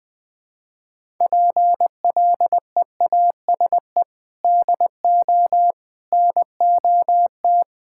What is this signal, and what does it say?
Morse code, heard as a keyed tone, about 21 wpm, 700 Hz. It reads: PLEASE DO NOT
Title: Sentences with 4-Word Phrases - 20wpm